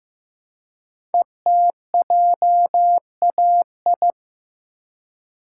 Morse code, heard as a keyed tone, about 15 wpm, 700 Hz. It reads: ETJAI
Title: Code Groups - 5 Letters - 15wpm